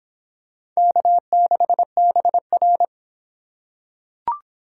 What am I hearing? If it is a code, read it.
Morse code, heard as a keyed tone, about 26 wpm, 700 Hz. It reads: K6BR